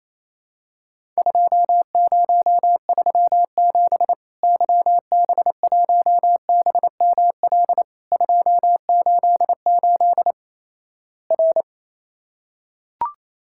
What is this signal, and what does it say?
Morse code, heard as a keyed tone, about 28 wpm, 700 Hz. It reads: 2037 YB1BML 288 R
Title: Worked All Europe DX Contest - QTC Exchange - 28wpm